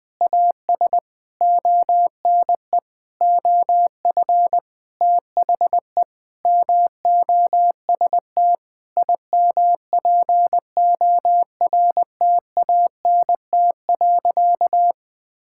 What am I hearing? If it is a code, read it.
Morse code, heard as a keyed tone, about 20 wpm, 700 Hz. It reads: AS ONE OF THE MOST IMPORTANT.
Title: Sentences with 9-Word Phrases - 20wpm